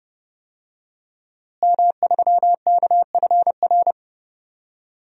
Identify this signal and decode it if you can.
Morse code, heard as a keyed tone, about 30 wpm, 700 Hz. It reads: M3KFR
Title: DXCC-100 Call Signs - 5 Characters - Repeated 2x - 30wpm